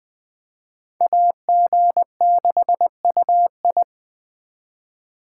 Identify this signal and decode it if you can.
Morse code, heard as a keyed tone, about 24 wpm, 700 Hz. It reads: AG6UI
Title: Speed Racing - USA Call Signs 20wpm